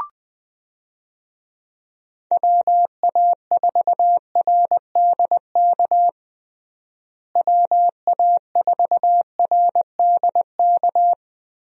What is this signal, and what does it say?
Morse code, heard as a keyed tone, about 20 wpm, 700 Hz. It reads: WA4RDK WA4RDK
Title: USA Call Signs - 6 Characters - Repeated 3x - 20wpm